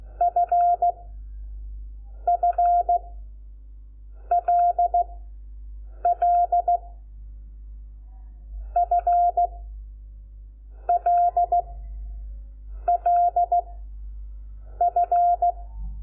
Morse code, heard as a keyed tone, about 16 wpm, 700 Hz. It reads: F F L L F L L F
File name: Lecciones / CLASE Nº5